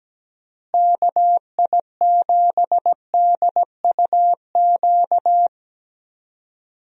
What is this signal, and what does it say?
Morse code, heard as a keyed tone, about 17 wpm, 700 Hz. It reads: KI7DUQ
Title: USA Call Signs - 6 Characters - Repeated 3x - 17wpm